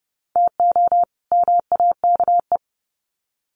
Morse code, heard as a keyed tone, about 30 wpm, 700 Hz. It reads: TO MAKE
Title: Sets of 2 Words 30wpm